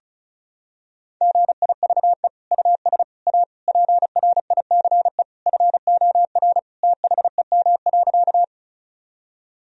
Morse code, heard as a keyed tone, about 36 wpm, 700 Hz. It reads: GIVE US A PRICE FOR THEM.
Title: Sentences from Top 1000 Words - Review Part 4 - 35wpm